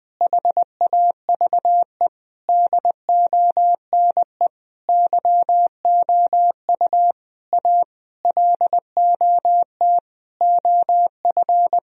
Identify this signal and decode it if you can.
Morse code, heard as a keyed tone, about 20 wpm, 700 Hz. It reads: HAVE DONE YOU A LOT OF
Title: Sentences from Top 1000 Words - Review Part 3 - 20wpm